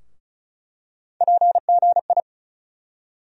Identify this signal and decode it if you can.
Morse code, heard as a keyed tone, about 34 wpm, 700 Hz. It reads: PGI
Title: USA Call Sign Postfixes - 3 Letters 35wpm